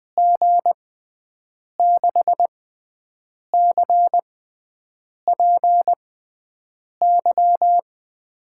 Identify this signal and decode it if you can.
Morse code, heard as a keyed tone, about 20 wpm, spaced as for 7 wpm, 700 Hz. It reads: G6CPY
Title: USA Call Signs - 6 Characters - 6x Character Spacing - 20wpm